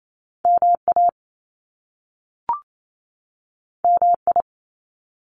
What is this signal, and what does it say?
Morse code, heard as a keyed tone, about 28 wpm, 700 Hz. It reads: MA MI
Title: US States 28wpm